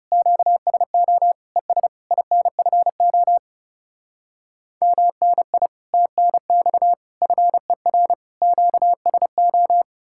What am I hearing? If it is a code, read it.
Morse code, heard as a keyed tone, about 40 wpm, 700 Hz. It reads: QSO ES INFO MNI TNX FER QSO